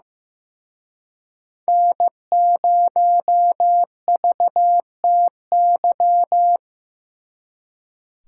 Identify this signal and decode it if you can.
Morse code, heard as a keyed tone, about 18 wpm, 700 Hz. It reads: N0VTY
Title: Speed Racing - USA Call Signs 15wpm